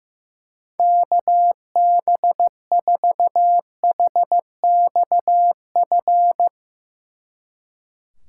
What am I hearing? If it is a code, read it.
Morse code, heard as a keyed tone, about 17 wpm, 700 Hz. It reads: KB4HXF
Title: Speed Racing - USA 2x3 Call Signs 15wpm